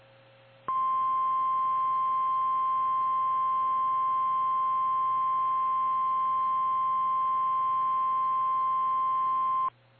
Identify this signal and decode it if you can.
Emergency Alert System (SAME) tones (the 1050 Hz attention tone). It is heard from 0.7 to 9.7 s.
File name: eas / KKPRKXPN / KKPRKXPN